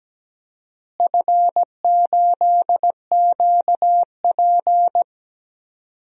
Morse code, heard as a keyed tone, about 17 wpm, 700 Hz. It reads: F8QP
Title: DXCC-100 Call Signs - 4 Characters - Repeated 3x - 17wpm